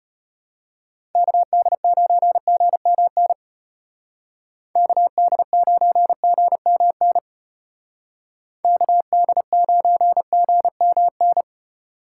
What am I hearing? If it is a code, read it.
Morse code, heard as a keyed tone, about 33 wpm, 700 Hz. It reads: KD9GMN KD9GMN KD9GMN